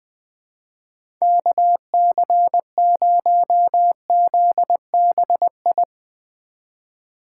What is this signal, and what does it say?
Morse code, heard as a keyed tone, about 19 wpm, 700 Hz. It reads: KC0ZBI
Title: Speed Racing - USA 2x3 Call Signs 17wpm